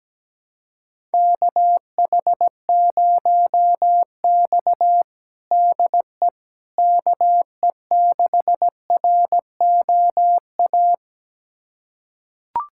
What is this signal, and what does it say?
Morse code, heard as a keyed tone, about 17 wpm, 700 Hz. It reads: KH0X DE KE6ROA
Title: USA Callsign DE Callsign 17wpm